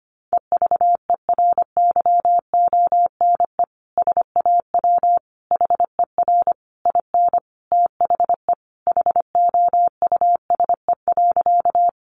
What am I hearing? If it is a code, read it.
Morse code, heard as a keyed tone, about 25 wpm, 700 Hz. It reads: EVERYONE SAW HER IN THE HOUSE.
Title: Sentences from Top 800 Words 25wpm